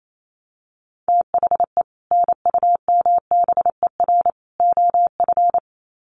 Morse code, heard as a keyed tone, about 28 wpm, 700 Hz. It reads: THE NUMBER OF
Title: Sets of 3 Words 28wpm